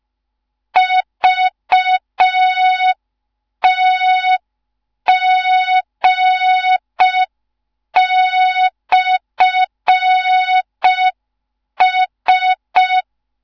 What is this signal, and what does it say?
Morse code, heard as a keyed tone, about 5 wpm, 3700 Hz. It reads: VTG/S
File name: Morze-HA2MN / hang / 15 GYAKORLAT